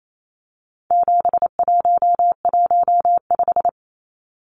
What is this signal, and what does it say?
Morse code, heard as a keyed tone, about 28 wpm, 700 Hz. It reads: 7115
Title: Numbers - 4-Digit 28wpm